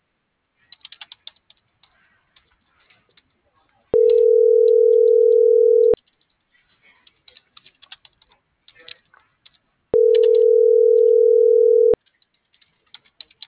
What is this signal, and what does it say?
Ringback tone (North America), 440 + 480 Hz.